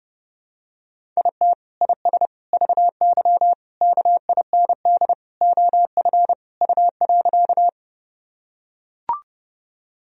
Morse code, heard as a keyed tone, about 30 wpm, 700 Hz. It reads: IT IS VY KIND OF U.
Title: Sentences from Top 200 Words and Abbreviations 30wpm